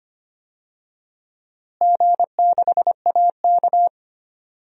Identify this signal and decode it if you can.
Morse code, heard as a keyed tone, about 25 wpm, 700 Hz. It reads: G6AK
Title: DXCC-100 Call Signs - 4 Characters - Repeated 2x - 25wpm